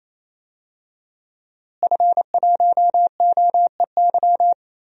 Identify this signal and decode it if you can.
Morse code, heard as a keyed tone, about 28 wpm, 700 Hz. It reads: F1OEY
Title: DXCC-100 Call Signs - 5 Characters - Repeated 3x - 28wpm